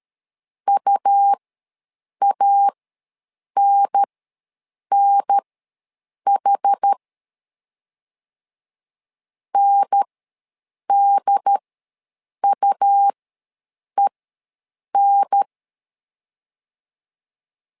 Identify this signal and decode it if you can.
Morse code, heard as a keyed tone, about 13 wpm, spaced as for 7 wpm, 800 Hz. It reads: UANNH NDUEN
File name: LEZIONI / 2 UHDN / 2 UHDN ws 12wpm 1708